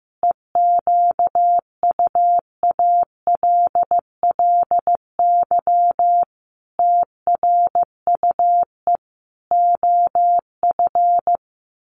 Morse code, heard as a keyed tone, about 15 wpm, 700 Hz. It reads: EQUALLY TRUE OF
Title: Sentences from Top 1400 Words 15wpm